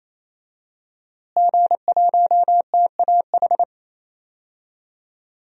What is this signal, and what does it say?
Morse code, heard as a keyed tone, about 28 wpm, 700 Hz. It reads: G1TAH
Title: DXCC-100 Call Signs - 5 Characters - Repeated 3x - 28wpm